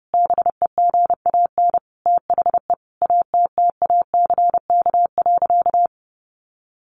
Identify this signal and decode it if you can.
Morse code, heard as a keyed tone, about 30 wpm, 700 Hz. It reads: BEGAN THE ATTACK.
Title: Sentences from Top 1500 Words 30wpm